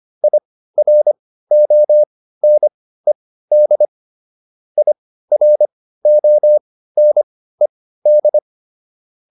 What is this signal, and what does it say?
Morse code, heard as a keyed tone, about 25 wpm, spaced as for 15 wpm, 600 Hz. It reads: IRONED IRONED